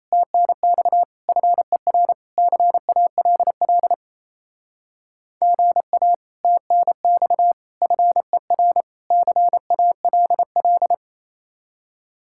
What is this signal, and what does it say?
Morse code, heard as a keyed tone, about 34 wpm, 700 Hz. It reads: TNX FER CALL GA TNX FER CALL
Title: Speed-Racing Common QSO Phrases - 28wpm